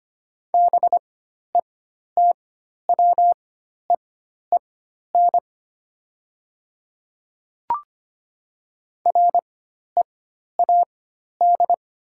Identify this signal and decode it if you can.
Morse code, heard as a keyed tone, about 25 wpm, spaced as for 12 wpm, 700 Hz. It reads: BETWEEN READ
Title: 4x Character Spacing - Top 500 Words - 25wpm